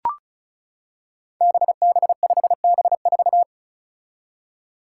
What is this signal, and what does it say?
Morse code, heard as a keyed tone, about 34 wpm, 700 Hz. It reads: BB5B4
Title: Five Challenging Letter-Number Combinations 35wpm